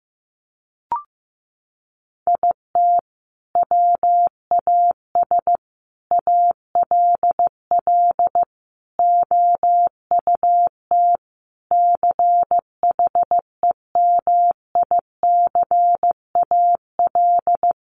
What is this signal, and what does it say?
Morse code, heard as a keyed tone, about 15 wpm, 700 Hz. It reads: IT WAS ALL OUT CHEMICAL
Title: Sentences from Top 1200 Words 15wpm